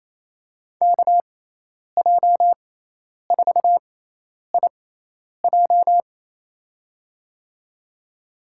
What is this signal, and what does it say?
Morse code, heard as a keyed tone, about 28 wpm, spaced as for 10 wpm, 700 Hz. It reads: KJ4IJ
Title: USA Call Signs - 5 Characters - 6x Character Spacing - 28wpm